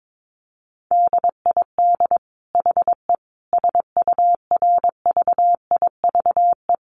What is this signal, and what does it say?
Morse code, heard as a keyed tone, about 22 wpm, 700 Hz. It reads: DID HE SURVIVE
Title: Sentences from Top 1300 Words 22wpm